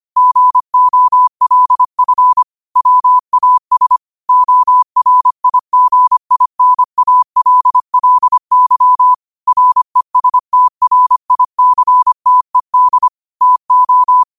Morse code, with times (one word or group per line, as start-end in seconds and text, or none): 0.2-2.4 GOLF
2.8-4.0 WAS
4.3-9.1 ORIGINALLY
9.5-13.1 RESTRICTED
13.4-14.2 TO